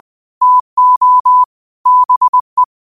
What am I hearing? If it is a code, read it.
Morse code, heard as a keyed tone, about 20 wpm, 1000 Hz. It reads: TO BE
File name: qotdfiles / 20wpm